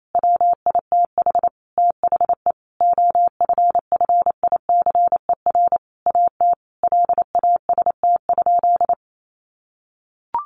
Morse code, heard as a keyed tone, about 28 wpm, 700 Hz. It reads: WITH THE OFFICER AT LAST?
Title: Sentences from Top 1300 Words 28wpm